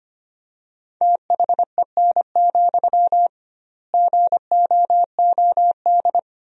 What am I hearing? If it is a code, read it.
Morse code, heard as a keyed tone, about 25 wpm, 700 Hz. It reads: THEN, GOOD
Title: 2x Word-Spacing - Sentences from Top 300 Words - 25wpm